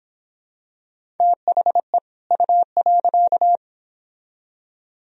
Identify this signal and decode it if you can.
Morse code, heard as a keyed tone, about 25 wpm, 700 Hz. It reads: THE U.